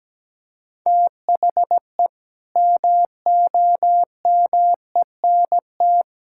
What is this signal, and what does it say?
Morse code, heard as a keyed tone, about 17 wpm, 700 Hz. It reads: THE MOMENT
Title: Sentences from Top 1000 Words - Review Part 5 - 17wpm